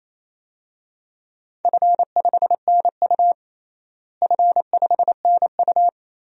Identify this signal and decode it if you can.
Morse code, heard as a keyed tone, about 28 wpm, 700 Hz. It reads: F5NU F5NU
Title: DXCC-100 Call Signs - 4 Characters - Repeated 2x - 28wpm